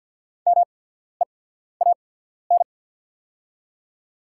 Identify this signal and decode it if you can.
Morse code, heard as a keyed tone, about 48 wpm, spaced as for 14 wpm, 690 Hz. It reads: MEAN
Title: 8x Character Spacing - Top 300 Words - 50wpm